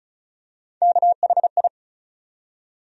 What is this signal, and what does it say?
Morse code, heard as a keyed tone, about 36 wpm, 700 Hz. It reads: KHI
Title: USA Call Sign Postfixes - 3 Letters 35wpm